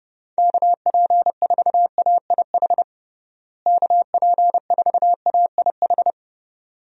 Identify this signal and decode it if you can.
Morse code, heard as a keyed tone, about 30 wpm, 700 Hz. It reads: KP4AIH KP4AIH
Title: DXCC-100 Call Signs - 6 Characters - Repeated 2x - 30wpm